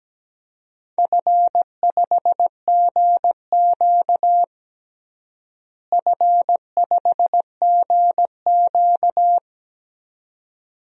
Morse code, heard as a keyed tone, about 17 wpm, 700 Hz. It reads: F5GQ F5GQ
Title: DXCC-100 Call Signs - 4 Characters - Repeated 3x - 17wpm